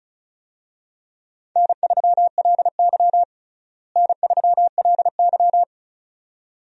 Morse code, heard as a keyed tone, about 36 wpm, 700 Hz. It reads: N3LY N3LY
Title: USA Call Signs - 4 Characters - Repeated 2x - 35wpm